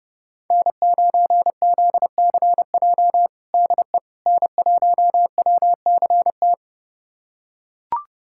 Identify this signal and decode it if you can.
Morse code, heard as a keyed tone, about 31 wpm, 700 Hz. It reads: N9ZCJ DE N1WCT
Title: USA Callsign DE Callsign - 5 Characters - 30wpm